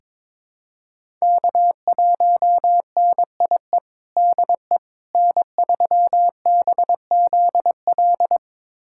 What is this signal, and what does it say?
Morse code, heard as a keyed tone, about 21 wpm, 700 Hz. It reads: K1NIE DE N3BZL
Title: USA Callsign DE Callsign - 5 Characters - 22wpm